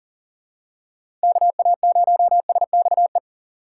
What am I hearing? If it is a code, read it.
Morse code, heard as a keyed tone, about 40 wpm, 700 Hz. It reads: KA0SXE
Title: USA Call Signs - 6 Characters - 6x Character Spacing - 40wpm